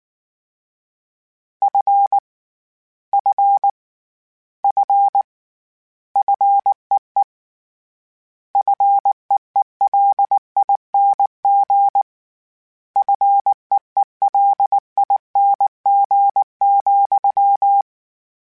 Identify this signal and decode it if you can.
Morse code, heard as a keyed tone, about 19 wpm, 800 Hz. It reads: F F F FEE FEELING FEELING,
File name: Short Runs / Short Runs-mp3s / SinglePrefixes @18